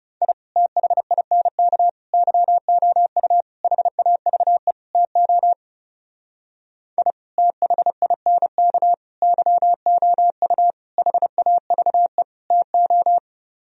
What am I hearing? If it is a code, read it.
Morse code, heard as a keyed tone, about 37 wpm, 700 Hz. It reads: I THINK YOU HAVE TO I THINK YOU HAVE TO
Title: Speed-Racing - Sets of 5 Words 30wpm